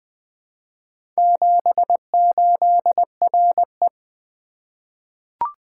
Morse code, heard as a keyed tone, about 20 wpm, 700 Hz. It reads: 78RE
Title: Four Letter-Number Combinations - 20wpm